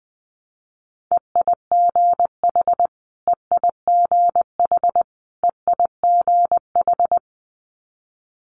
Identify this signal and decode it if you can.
Morse code, heard as a keyed tone, about 20 wpm, 700 Hz. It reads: EIGH EIGH EIGH
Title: Multi-letter Phonemes Repeated 3x 20wpm